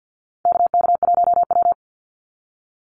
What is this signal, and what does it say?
Morse code, heard as a keyed tone, about 50 wpm, 700 Hz. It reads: KK1W
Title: NA SOTA Chaser Callsigns 50wpm